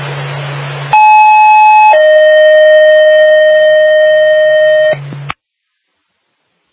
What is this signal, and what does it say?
Two-tone page, heard as a one-second tone, then a three-second tone, about 860 then 620 Hz. It is heard from 0.9 to 4.9 s.